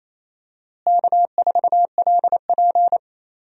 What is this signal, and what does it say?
Morse code, heard as a keyed tone, about 28 wpm, 700 Hz. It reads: K4LP
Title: USA Call Signs - 4 Characters - Repeated 2x - 28wpm